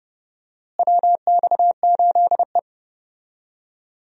Morse code, heard as a keyed tone, about 30 wpm, 700 Hz. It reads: WX8E
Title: USA Call Signs - 4 Characters - Repeated 2x - 30wpm